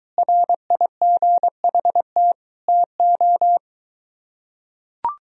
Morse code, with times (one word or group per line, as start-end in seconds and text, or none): 0.2-2.3 RIGHT
2.7-3.6 TO